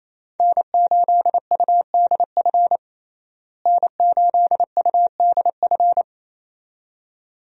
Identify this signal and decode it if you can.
Morse code, heard as a keyed tone, about 28 wpm, 700 Hz. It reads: N8UDF N8UDF
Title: USA Call Signs - 5 Characters - Repeated 2x - 28wpm